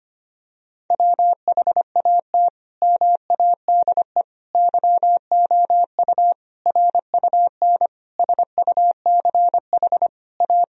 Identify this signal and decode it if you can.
Morse code, heard as a keyed tone, about 25 wpm, 700 Hz. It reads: WHAT MADE YOU RUN SUCH A
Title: Sentences from Top 1000 Words - Review Part 4 - 25wpm